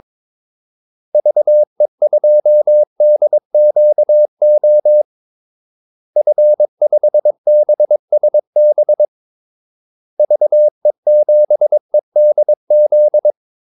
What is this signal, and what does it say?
Morse code, heard as a keyed tone, about 22 wpm, 600 Hz. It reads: VE2DQO F5BSB VE7EDZ
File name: Callsigns-22wpm